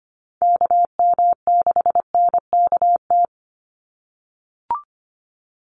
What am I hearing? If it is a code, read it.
Morse code, heard as a keyed tone, about 25 wpm, 700 Hz. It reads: KM6NKT
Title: USA Call Signs 25wpm